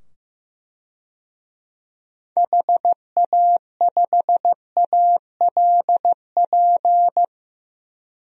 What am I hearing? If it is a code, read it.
Morse code, heard as a keyed tone, about 15 wpm, 700 Hz. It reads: HA5ALP
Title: DXCC-100 Call Signs - 6 Characters - 15wpm